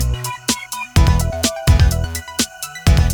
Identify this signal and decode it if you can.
There is no code.